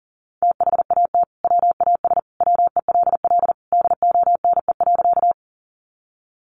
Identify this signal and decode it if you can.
Morse code, heard as a keyed tone, about 40 wpm, 700 Hz. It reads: THAT WAS WELL DONE.
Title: Sentences from Top 500 Words 40wpm